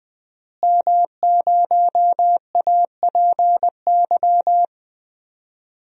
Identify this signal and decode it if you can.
Morse code, heard as a keyed tone, about 20 wpm, 700 Hz. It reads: M0APY
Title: DXCC-100 Call Signs - 5 Characters - Repeated 2x - 20wpm